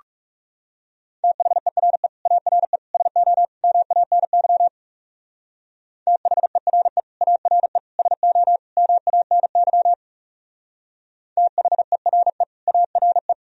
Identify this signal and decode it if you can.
Morse code, heard as a keyed tone, about 38 wpm, 700 Hz. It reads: THERE ARE SO MANY THERE ARE SO MANY THERE ARE